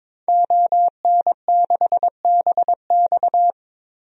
Morse code, heard as a keyed tone, about 22 wpm, 700 Hz. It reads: ON6BX
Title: DXCC-100 Call Signs - 5 Characters - 22wpm